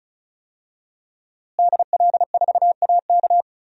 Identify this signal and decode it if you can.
Morse code, heard as a keyed tone, about 36 wpm, 700 Hz. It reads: DL4AK
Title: DXCC-100 Call Signs - 5 Characters - Repeated 2x - 35wpm